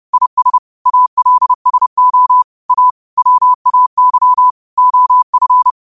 Morse code, heard as a keyed tone, about 30 wpm, 1000 Hz. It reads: IS ALSO A WAY OF
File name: qotdfiles / 30wpm